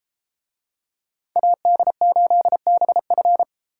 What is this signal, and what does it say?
Morse code, heard as a keyed tone, about 33 wpm, 700 Hz. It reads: AD8BF